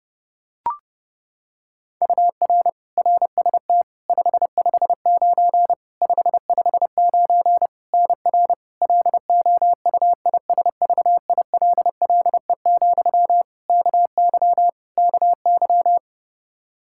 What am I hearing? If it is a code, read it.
Morse code, heard as a keyed tone, about 30 wpm, 700 Hz. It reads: UR RST 559 559 NR LOUISVILLE, KY KY
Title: Practice QSO - Exchange 1 - Beginning - 30wpm